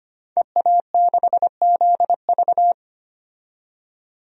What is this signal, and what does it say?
Morse code, heard as a keyed tone, about 25 wpm, 700 Hz. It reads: EA6ZV
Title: DXCC-100 Call Signs - 5 Characters - 25wpm